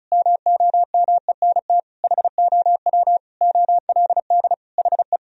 Morse code, heard as a keyed tone, about 36 wpm, 700 Hz. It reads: MOMENT HOW OLD HE
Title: Sentences from Top 1000 Words - Review Part 4 - 35wpm